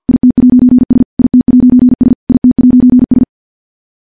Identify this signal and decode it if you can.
Morse code, heard as a keyed tone, about 50 wpm, 250 Hz. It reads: IT'S IT'S IT'S
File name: morse / morse-training-50WPM